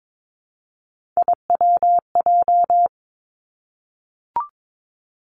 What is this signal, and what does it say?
Morse code, heard as a keyed tone, about 22 wpm, 700 Hz. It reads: IWJ
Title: Three Challenging Letter-Number Combinations 22wpm